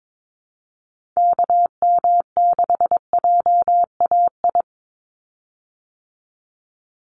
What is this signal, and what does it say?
Morse code, heard as a keyed tone, about 22 wpm, 700 Hz. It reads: KM6JAI
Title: USA Call Signs 22wpm